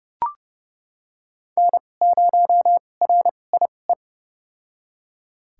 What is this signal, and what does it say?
Morse code, heard as a keyed tone, about 30 wpm, spaced as for 22 wpm, 700 Hz. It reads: N0RIE
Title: USA Call Signs - 5 Characters - 2x Character Spacing - 30wpm